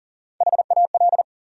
Morse code, heard as a keyed tone, about 40 wpm, 700 Hz. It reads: HAL
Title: USA Common Names - Max 5 Letters - 40wpm